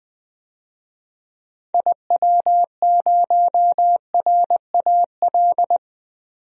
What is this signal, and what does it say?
Morse code, heard as a keyed tone, about 20 wpm, 700 Hz. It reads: IW0RAL
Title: DXCC-100 Call Signs - 6 Characters - 20wpm